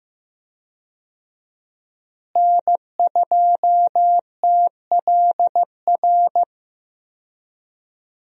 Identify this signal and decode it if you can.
Morse code, heard as a keyed tone, about 16 wpm, 700 Hz. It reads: N2TLR
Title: USA Call Signs - 5 Characters - Repeated 2x - 15wpm